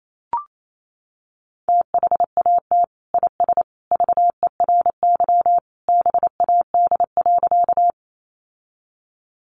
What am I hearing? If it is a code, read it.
Morse code, heard as a keyed tone, about 28 wpm, 700 Hz. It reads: THAT IS VERY BAD.
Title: Sentences from Top 700 Words 28wpm